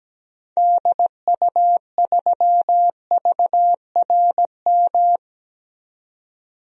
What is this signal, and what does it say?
Morse code, heard as a keyed tone, about 17 wpm, 700 Hz. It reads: DU3VRM
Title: DXCC-100 Call Signs - 6 Characters - 17wpm